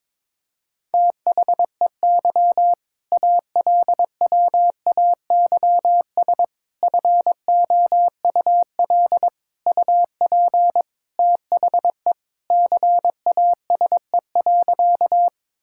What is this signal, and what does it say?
Morse code, heard as a keyed tone, about 22 wpm, 700 Hz. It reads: THEY ALWAYS FOUL UP THE CASE.
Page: morse-code-ninja-practice-22wpm